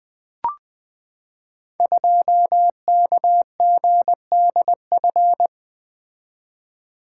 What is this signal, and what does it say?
Morse code, heard as a keyed tone, about 20 wpm, 700 Hz. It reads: 2KGDF
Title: Code Groups - 5 Letter:Numbers - 20wpm